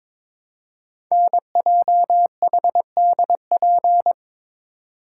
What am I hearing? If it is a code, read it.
Morse code, heard as a keyed tone, about 22 wpm, 700 Hz. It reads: NJHDP
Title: Code Groups - 5 Letters - 22wpm